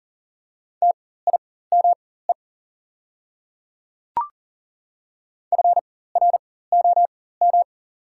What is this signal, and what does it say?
Morse code, heard as a keyed tone, about 42 wpm, spaced as for 19 wpm, 690 Hz. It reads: TIME FROM
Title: 4x Character Spacing - Top 100 Words - 40wpm